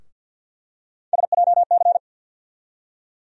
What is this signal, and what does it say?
Morse code, heard as a keyed tone, about 50 wpm, 700 Hz. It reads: SJC
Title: USA Call Sign Postfixes - 3 Letters 50wpm